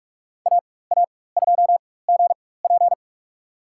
Morse code, heard as a keyed tone, about 42 wpm, spaced as for 20 wpm, 690 Hz. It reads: AA2GP
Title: USA Call Signs - 5 Characters - 4x Character Spacing - 45wpm